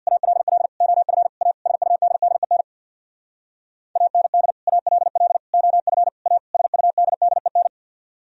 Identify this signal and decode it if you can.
Morse code, heard as a keyed tone, about 64 wpm, 690 Hz. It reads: ALL OF A SUDDEN AND ALL OF A SUDDEN